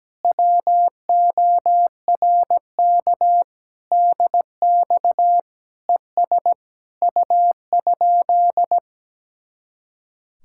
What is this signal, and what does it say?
Morse code, heard as a keyed tone, about 17 wpm, 700 Hz. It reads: WORK DX ES U?
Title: Speed-Racing Common QSO Phrases Encore - 17wpm